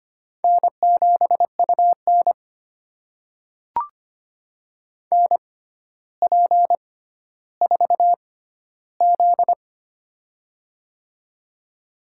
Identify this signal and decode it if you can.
Morse code, heard as a keyed tone, about 25 wpm, spaced as for 13 wpm, 700 Hz. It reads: N7UN NP4Z
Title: USA Call Signs - 4 Characters - 6x Character Spacing - 25wpm